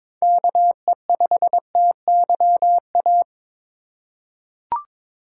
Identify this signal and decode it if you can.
Morse code, heard as a keyed tone, about 21 wpm, 700 Hz. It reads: KE5TYA
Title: USA Call Signs - 6 Characters - 8x Character Spacing - 22wpm